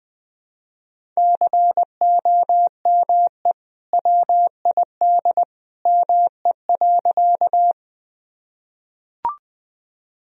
Morse code, heard as a keyed tone, about 20 wpm, 700 Hz. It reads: COME WID ME.
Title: Sentences from Top 200 Words and Abbreviations 20wpm